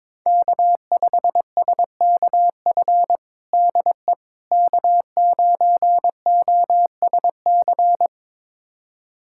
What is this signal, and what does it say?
Morse code, heard as a keyed tone, about 22 wpm, 700 Hz. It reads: K5SKF DE K9OSC
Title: USA Callsign DE Callsign - 5 Characters - 22wpm